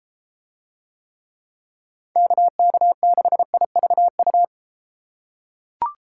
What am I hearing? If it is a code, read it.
Morse code, heard as a keyed tone, about 31 wpm, 700 Hz. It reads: KK6IVU